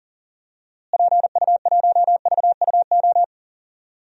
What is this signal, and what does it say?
Morse code, heard as a keyed tone, about 41 wpm, 700 Hz. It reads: PU1VUO
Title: DXCC-100 Call Signs - 6 Characters - Repeated 3x - 40wpm